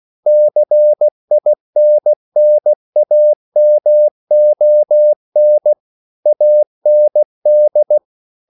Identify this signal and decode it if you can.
Morse code, heard as a keyed tone, about 16 wpm, 600 Hz. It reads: CINNAMON AND